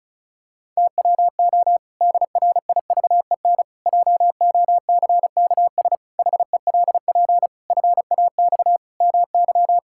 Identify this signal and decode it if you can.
Morse code, heard as a keyed tone, about 36 wpm, 700 Hz. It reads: TWO DRIVEN JOCKS HELP FAX MY
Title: Pangram Challenge v2 - 35wpm